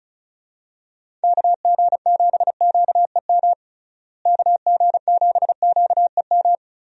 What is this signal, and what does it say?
Morse code, heard as a keyed tone, about 36 wpm, 700 Hz. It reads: KG7QEM KG7QEM
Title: USA Call Signs - 6 Characters - Repeated 3x - 35wpm